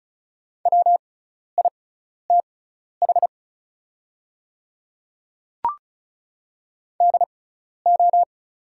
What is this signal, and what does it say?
Morse code, heard as a keyed tone, about 34 wpm, spaced as for 12 wpm, 700 Hz. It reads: WITH DO
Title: 6x Character Spacing - Top 100 Words - 35wpm